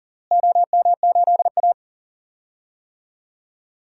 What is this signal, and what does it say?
Morse code, heard as a keyed tone, about 37 wpm, 700 Hz. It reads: OM8A
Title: DXCC-100 Call Signs - 4 Characters - Repeated 2x - 40wpm